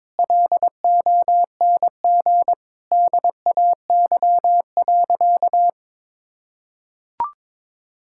Morse code, heard as a keyed tone, about 22 wpm, 700 Hz. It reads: LONG DAY.
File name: practice / 22wpm / Sentences with 2-Word Phrases - 22wpm